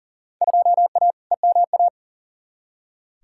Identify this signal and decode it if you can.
Morse code, heard as a keyed tone, about 40 wpm, 700 Hz. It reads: 2A EMA
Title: Field Day Exchanges Encore - 40wpm